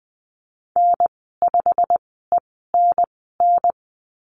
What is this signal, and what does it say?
Morse code, heard as a keyed tone, about 20 wpm, spaced as for 14 wpm, 700 Hz. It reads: N5ENN
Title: 128 Lesson 21g - Number 6 - Call Signs 20wpm